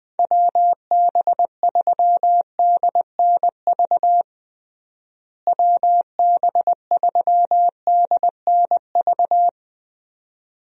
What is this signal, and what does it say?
Morse code, heard as a keyed tone, about 20 wpm, 700 Hz. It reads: WB3DNV WB3DNV
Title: USA Call Signs - 6 Characters - Repeated 2x - 20wpm